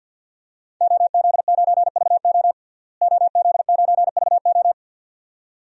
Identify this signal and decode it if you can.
Morse code, heard as a keyed tone, about 49 wpm, 700 Hz. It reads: OZ9VO OZ9VO
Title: DXCC-100 Call Signs - 5 Characters - Repeated 2x - 50wpm